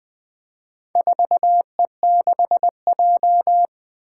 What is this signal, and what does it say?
Morse code, heard as a keyed tone, about 20 wpm, 700 Hz. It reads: 4E6J
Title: Four Challenging Letter-Number Combinations 20wpm